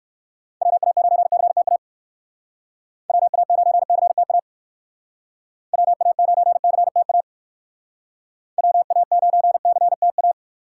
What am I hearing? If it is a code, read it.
Morse code, heard as a keyed tone, about 60 wpm, 700 Hz. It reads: WA9CTA WA9CTA WA9CTA WA9CTA